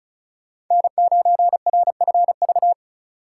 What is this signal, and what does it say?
Morse code, heard as a keyed tone, about 46 wpm, 700 Hz. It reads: N9RFV